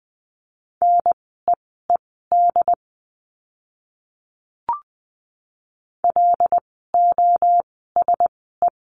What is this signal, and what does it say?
Morse code, heard as a keyed tone, about 20 wpm, spaced as for 15 wpm, 700 Hz. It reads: NEED LOSE
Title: 094 Lesson 16f - Letter c - Words 20wpm